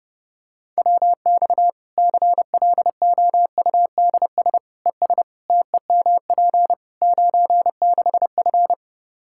Morse code, heard as a keyed tone, about 32 wpm, 700 Hz. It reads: WX CLOUDS ES TEMP 96F
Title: Practice QSO - Exchange 2 - Info - 30wpm